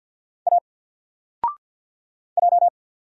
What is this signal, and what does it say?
Morse code, heard as a keyed tone, about 48 wpm, 700 Hz. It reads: A J
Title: Single Character (Letter:Number:Punctuation:Prosign) Alt 50wpm